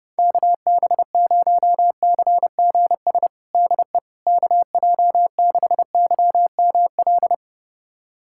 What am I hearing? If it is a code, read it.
Morse code, heard as a keyed tone, about 32 wpm, 700 Hz. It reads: KB0CGS DE KJ6YML
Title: USA Callsign DE Callsign - 6 Characters - 30wpm